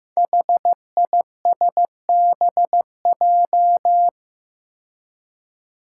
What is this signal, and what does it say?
Morse code, heard as a keyed tone, about 15 wpm, 700 Hz. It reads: HISBJ
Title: Five Challenging Letter-Number Combinations 15wpm